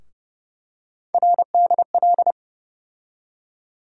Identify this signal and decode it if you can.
Morse code, heard as a keyed tone, about 30 wpm, 700 Hz. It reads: RDL
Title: USA Call Sign Postfixes - 3 Letters 30wpm